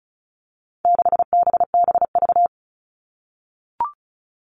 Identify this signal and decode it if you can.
Morse code, heard as a keyed tone, about 34 wpm, 700 Hz. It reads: 6BBV
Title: All Cut Numbers - 4-Digit 35wpm